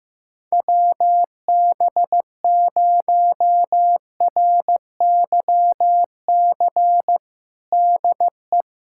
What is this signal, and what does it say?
Morse code, heard as a keyed tone, about 15 wpm, 700 Hz. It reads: WB0RYC DE
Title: USA Callsign DE Callsign 15wpm